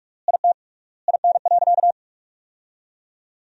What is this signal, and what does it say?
Morse code, heard as a keyed tone, about 46 wpm, 700 Hz. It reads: IT IN.
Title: 3x Word-Spacing - Sentences from Top 200 Words - 45wpm